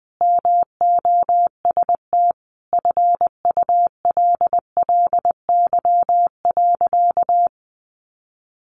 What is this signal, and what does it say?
Morse code, heard as a keyed tone, about 19 wpm, 700 Hz. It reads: MOST FULLY.
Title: Sentences from Top 1300 Words 20wpm